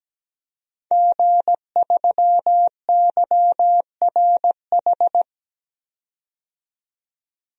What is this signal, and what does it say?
Morse code, heard as a keyed tone, about 17 wpm, 700 Hz. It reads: G3YRH
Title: DXCC-100 Call Signs - 5 Characters - 17wpm